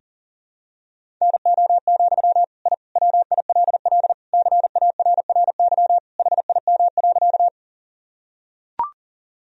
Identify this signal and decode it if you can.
Morse code, heard as a keyed tone, about 40 wpm, 700 Hz. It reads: NO, I WILL CARRY HIM.
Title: Sentences from Top 1000 Words - Review Part 1 - 40wpm